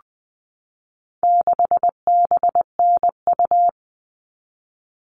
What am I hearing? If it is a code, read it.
Morse code, heard as a keyed tone, about 20 wpm, 700 Hz. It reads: 6BNU
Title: All Cut Numbers - 4-Digit 20wpm